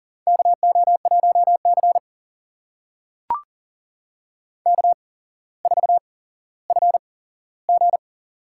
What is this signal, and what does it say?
Morse code, heard as a keyed tone, about 40 wpm, spaced as for 17 wpm, 700 Hz. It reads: KO1C K4FG
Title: USA Call Signs - 4 Characters - 8x Character Spacing - 40wpm